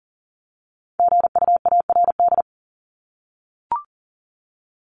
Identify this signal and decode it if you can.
Morse code, heard as a keyed tone, about 40 wpm, 700 Hz. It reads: GUARD
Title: Top 501-750 Words 40wpm